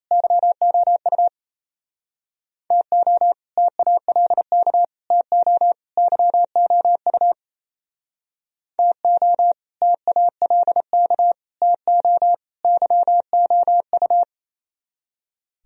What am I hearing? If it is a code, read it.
Morse code, heard as a keyed tone, about 34 wpm, 700 Hz. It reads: YOU TO TALK TO YOU TO TALK TO YOU
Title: Speed-Racing - Sets of 4 Words 28wpm